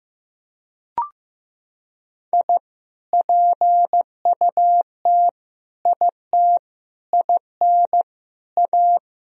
Morse code, heard as a keyed tone, about 15 wpm, 700 Hz. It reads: I PUT IT IN A
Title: Sentences from Top 1000 Words - Review Part 2 - 15wpm